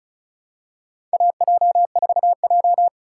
Morse code, heard as a keyed tone, about 34 wpm, 690 Hz. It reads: AJ4J
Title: USA Call Signs - 4 Characters - 2x Character Spacing - 35wpm